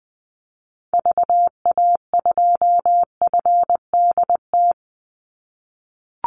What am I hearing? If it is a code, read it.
Morse code, heard as a keyed tone, about 20 wpm, 700 Hz. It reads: VA2FDT
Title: Call Signs - 20wpm